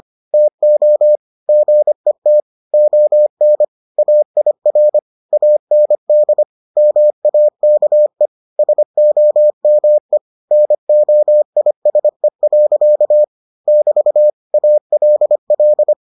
Morse code, heard as a keyed tone, about 25 wpm, 600 Hz. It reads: TO GET ON AIR AND MAKE SOME NOISE. <BT> ALL